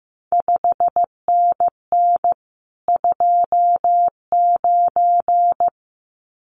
Morse code, heard as a keyed tone, about 15 wpm, 700 Hz. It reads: 5NN 29
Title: CQ WW DX Contest Exchange Encore - 15wpm